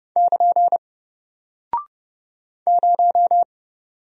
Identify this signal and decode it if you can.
Morse code, heard as a keyed tone, about 30 wpm, 700 Hz. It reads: <KN> 0
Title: Single Character (Letter:Number:Punctuation:Prosign) Alt 30wpm